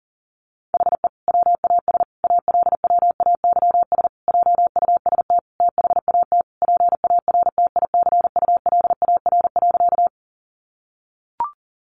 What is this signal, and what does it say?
Morse code, heard as a keyed tone, about 40 wpm, 700 Hz. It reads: HE WAS ALWAYS JUST THAT PARTICULAR.
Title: Sentences from Top 600 Words 40wpm